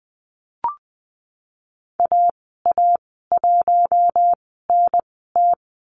Morse code, heard as a keyed tone, about 20 wpm, spaced as for 15 wpm, 700 Hz. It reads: AA1NT
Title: 048 Lesson 9g - Number 4 - Call Signs 20wpm